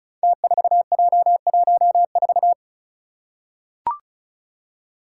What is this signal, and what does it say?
Morse code, heard as a keyed tone, about 36 wpm, 700 Hz. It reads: T4J14
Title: Five Challenging Letter-Number Combinations 35wpm